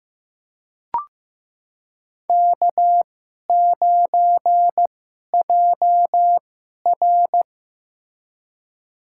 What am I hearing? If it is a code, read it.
Morse code, heard as a keyed tone, about 15 wpm, spaced as for 11 wpm, 700 Hz. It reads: K9JR
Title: USA Call Signs - 4 Characters - 2x Character Spacing - 15wpm